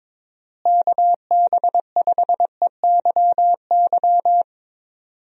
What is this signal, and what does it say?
Morse code, heard as a keyed tone, about 23 wpm, 700 Hz. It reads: KB5EYY
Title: USA Call Signs - 6 Characters - Repeated 2x - 22wpm